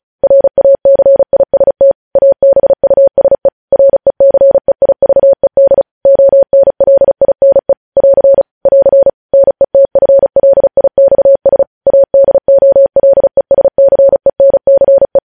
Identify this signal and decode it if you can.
Morse code, heard as a keyed tone, about 36 wpm, 550 Hz. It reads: RACIST ABUSE RECEIVED ONLINE <AR> <AR> NETFLIXS ADOLESCENCE
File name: podcast / News headlines-1x-35x35